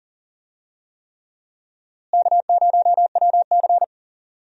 Morse code, heard as a keyed tone, about 40 wpm, 700 Hz. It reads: K0WC
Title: USA Call Signs - 4 Characters - Repeated 3x - 40wpm